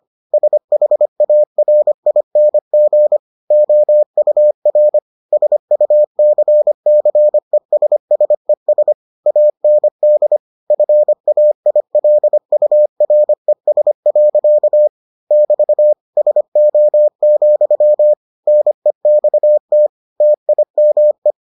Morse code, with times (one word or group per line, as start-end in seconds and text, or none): 0.3-3.2 SHARING
3.5-5.0 OUR
5.3-8.9 SUCCESSES
9.3-10.4 AND
10.7-14.9 FAILURES.
15.3-15.9 <BT>
16.2-18.1 SO,
18.5-19.9 NEXT
20.2-21.3 TIME